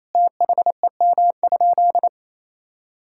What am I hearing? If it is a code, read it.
Morse code, heard as a keyed tone, about 28 wpm, 700 Hz. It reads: THEM?
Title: Sentences from Top 1000 Words - Review Part 6 - 28wpm